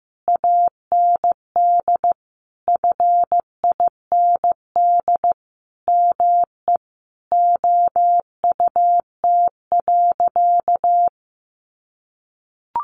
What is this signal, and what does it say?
Morse code, heard as a keyed tone, about 15 wpm, 700 Hz. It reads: AND FIND ME OUT.
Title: Sentences from Top 500 Words 15wpm